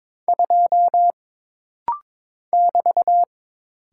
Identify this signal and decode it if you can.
Morse code, heard as a keyed tone, about 22 wpm, 700 Hz. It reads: 2 <BT>
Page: This rapid-fire timing is geared to push your ICR (Instant Character Recognition) skills.